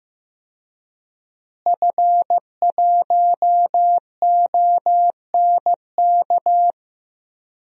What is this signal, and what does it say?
Morse code, heard as a keyed tone, about 15 wpm, 700 Hz. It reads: F1ONK
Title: DXCC-100 Call Signs - 5 Characters - Repeated 3x - 15wpm